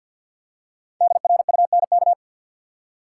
Morse code, heard as a keyed tone, about 51 wpm, 700 Hz. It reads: DRUNK
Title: Five Letter Words - 50wpm